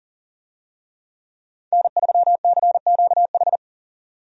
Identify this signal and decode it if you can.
Morse code, heard as a keyed tone, about 40 wpm, 700 Hz. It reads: N3CQH
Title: USA Call Signs - 5 Characters - 40wpm